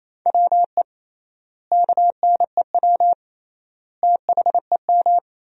Morse code, heard as a keyed tone, about 28 wpm, 700 Hz. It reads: WE KNEW THEM
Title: 3x Word-Spacing - Sentences from Top 400 Words - 28wpm